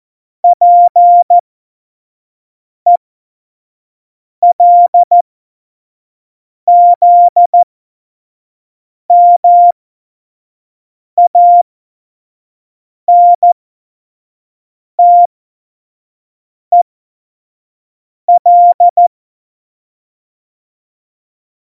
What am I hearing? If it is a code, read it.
Morse code, heard as a keyed tone, about 14 wpm, spaced as for 5 wpm, 700 Hz. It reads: PELZMANTEL
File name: Morsekurs / Witze / Witz 05. Juni 2018, 25 F70